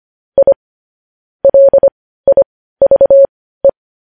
Morse code, heard as a keyed tone, about 25 wpm, spaced as for 15 wpm, 550 Hz. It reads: I LIVE
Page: news-headlines-in-morse-code-at-15-wpm